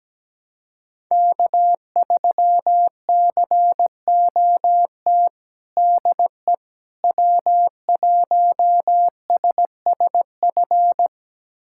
Morse code, heard as a keyed tone, about 17 wpm, 700 Hz. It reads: K3COT DE W1SSF
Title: USA Callsign DE Callsign 17wpm